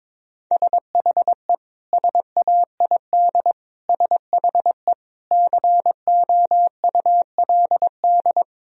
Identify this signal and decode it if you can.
Morse code, heard as a keyed tone, about 21 wpm, 700 Hz. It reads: SHE SAID SHE COULD
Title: Sentences from Top 1000 Words - Review Part 4 - 22wpm